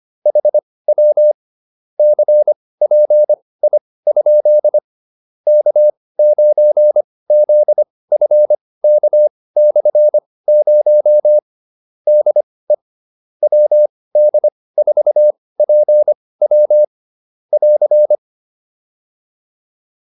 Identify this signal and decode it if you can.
Morse code, heard as a keyed tone, about 25 wpm, spaced as for 18 wpm, 600 Hz. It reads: HW CPI? K9ZFK/0 DE WD4PW <AR>
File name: QSO303 18